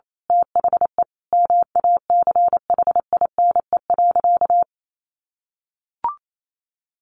Morse code, heard as a keyed tone, about 28 wpm, 700 Hz. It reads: THE MACHINE.
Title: A Sound Of Thunder 28wpm